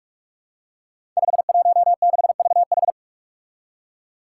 Morse code, heard as a keyed tone, about 46 wpm, 700 Hz. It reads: 516VH
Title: Five Challenging Letter-Number Combinations 45wpm